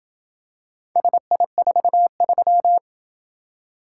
Morse code, heard as a keyed tone, about 26 wpm, 700 Hz. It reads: SI43